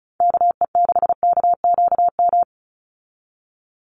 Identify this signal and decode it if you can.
Morse code, heard as a keyed tone, about 34 wpm, 700 Hz. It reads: KE6KQM
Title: USA Call Signs 35wpm